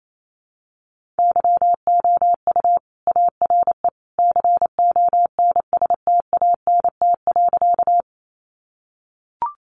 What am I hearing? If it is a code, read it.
Morse code, heard as a keyed tone, about 28 wpm, 700 Hz. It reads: YOU ARE CONSTANT.
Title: Sentences from Top 1500 Words 28wpm